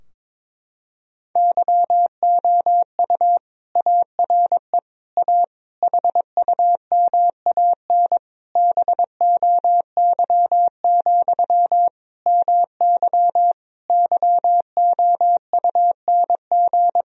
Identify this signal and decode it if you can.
Morse code, heard as a keyed tone, about 22 wpm, 700 Hz. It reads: YOU ARE A HUMAN BOY, MY YOUNG
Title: Sentences from Top 1000 Words - Review Part 3 - 22wpm